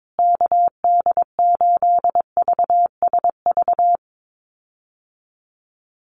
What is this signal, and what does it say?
Morse code, heard as a keyed tone, about 22 wpm, 700 Hz. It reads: KD8VSV
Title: USA Call Signs Repeated 3x 22wpm